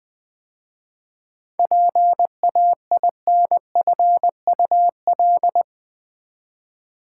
Morse code, heard as a keyed tone, about 20 wpm, 700 Hz. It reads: PAINFUL
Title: 7 Letter Words - First 3 Letters Given - 20wpm